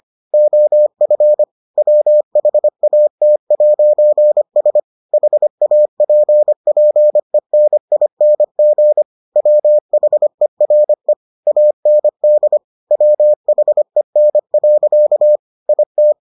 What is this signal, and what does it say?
Morse code, heard as a keyed tone, about 25 wpm, 600 Hz. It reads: OF WHAT'S HAPPENING WHERE AND WHEN. IT